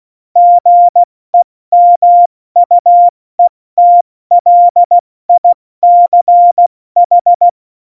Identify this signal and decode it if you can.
Morse code, heard as a keyed tone, about 16 wpm, spaced as for 14 wpm, 700 Hz. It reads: GEMUETLICH
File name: Morsekurs / Witze / Witz 08. Oktober 2019, 70 F80